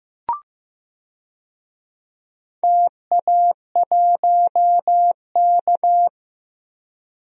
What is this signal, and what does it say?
Morse code, heard as a keyed tone, about 16 wpm, 700 Hz. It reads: TA1K
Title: DXCC-100 Call Signs - 4 Characters - 15wpm